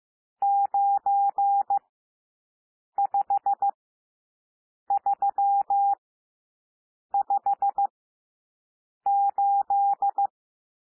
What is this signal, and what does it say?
Morse code, heard as a keyed tone, about 15 wpm, spaced as for 6 wpm, 800 Hz. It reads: 95358
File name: cwmp3 / N1 SOUND NUMBERS 15 wpm